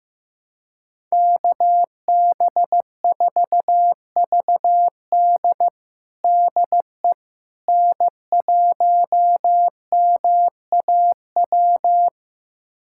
Morse code, heard as a keyed tone, about 15 wpm, 700 Hz. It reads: KB4VD DE N1MAW
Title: USA Callsign DE Callsign - 5 Characters - 15wpm